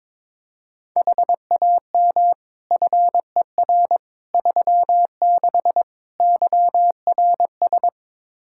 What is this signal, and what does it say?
Morse code, heard as a keyed tone, about 21 wpm, 700 Hz. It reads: HAM FER 36 YRS
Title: Practice QSO - Exchange 3 - More Info - 22wpm